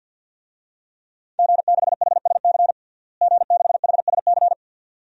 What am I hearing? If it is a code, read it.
Morse code, heard as a keyed tone, about 49 wpm, 700 Hz. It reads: G6HSC G6HSC
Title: DXCC-100 Call Signs - 5 Characters - Repeated 2x - 50wpm